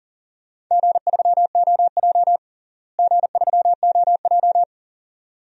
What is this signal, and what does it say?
Morse code, heard as a keyed tone, about 40 wpm, 700 Hz. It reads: G3OJ G3OJ
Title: DXCC-100 Call Signs - 4 Characters - Repeated 2x - 40wpm